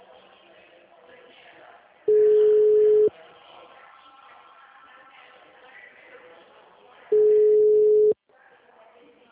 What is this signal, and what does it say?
Ringback tone (Europe, 425 Hz), with speech.
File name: out- -2015- - - .31039